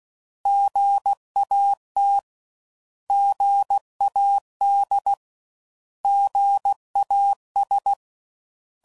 Morse code, heard as a keyed tone, about 16 wpm, 800 Hz. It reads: GAT GAD GAS
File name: Prefixes / By First Letter / Gs